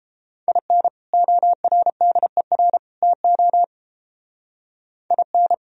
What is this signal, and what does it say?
Morse code, heard as a keyed tone, about 29 wpm, 700 Hz. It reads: IN ORDER TO IN